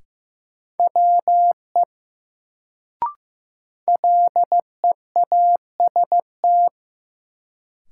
Morse code, heard as a keyed tone, about 15 wpm, 700 Hz. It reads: WE LEAST
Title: Sets of 2 Words - New Word Drill - 15wpm